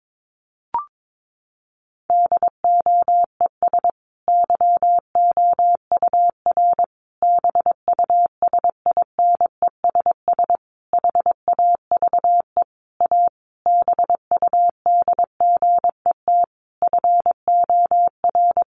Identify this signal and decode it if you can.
Morse code, heard as a keyed tone, about 22 wpm, 700 Hz. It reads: DOES YOUR BUSINESS HAVE A BUDGET FOR
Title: Sentences from Top 1200 Words 22wpm